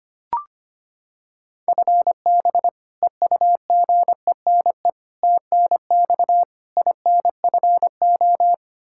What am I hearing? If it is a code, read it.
Morse code, heard as a keyed tone, about 25 wpm, 700 Hz. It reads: FB EUGENE TNX INFO
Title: Practice QSO - Exchange 3 - More Info - 25wpm